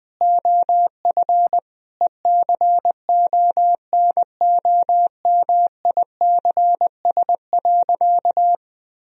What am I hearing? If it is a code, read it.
Morse code, heard as a keyed tone, about 20 wpm, 700 Hz. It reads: OF ECONOMICS.
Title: Sentences from Top 1000 Words - Review Part 3 - 20wpm